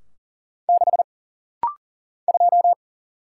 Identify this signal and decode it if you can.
Morse code, heard as a keyed tone, about 42 wpm, 700 Hz. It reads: 6 2
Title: Single Letter-Number Alt - Rapid-Fire 40wpm